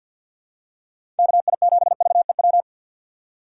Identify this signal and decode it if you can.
Morse code, heard as a keyed tone, about 51 wpm, 700 Hz. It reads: KI7VEW
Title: USA Call Signs - 6 Characters - 8x Character Spacing - 50wpm